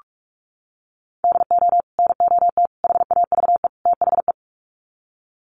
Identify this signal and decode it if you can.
Morse code, heard as a keyed tone, about 45 wpm, 700 Hz. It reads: DO NOT HAVE THE
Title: Sets of 4 Words 45wpm